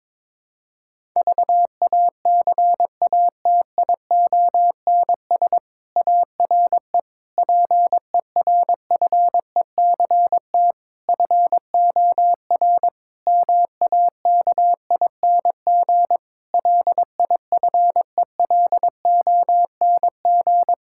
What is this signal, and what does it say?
Morse code, heard as a keyed tone, about 22 wpm, 700 Hz. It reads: VACATIONS ARE PERFECT FOR MAKING LIFELONG
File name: Difficult Sentences with Hints - 22wpm